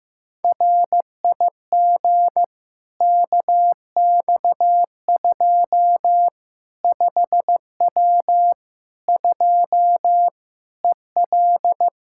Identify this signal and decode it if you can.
Morse code, heard as a keyed tone, about 15 wpm, 700 Hz. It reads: RIG KX2 5W 2 EL
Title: Practice QSO - Exchange 2 - Info - 15wpm